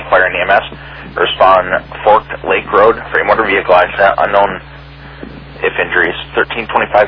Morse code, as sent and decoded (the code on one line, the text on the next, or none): none